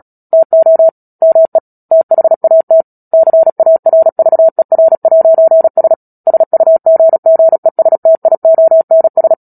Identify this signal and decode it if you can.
Morse code, heard as a keyed tone, about 37 wpm, 650 Hz. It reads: TO ME THAT CARVER'S SUGGESTIONS
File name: Book Two / 10-Chapter 10 - Epilogue - 36 WPM